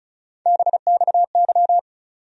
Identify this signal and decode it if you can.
Morse code, heard as a keyed tone, about 36 wpm, 700 Hz. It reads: BXY
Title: USA Call Sign Postfixes - 3 Letters 35wpm